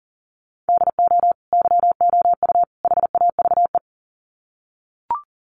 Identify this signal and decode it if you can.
Morse code, heard as a keyed tone, about 40 wpm, 700 Hz. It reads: DO YOU HAVE
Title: Sets of 3 Words 40wpm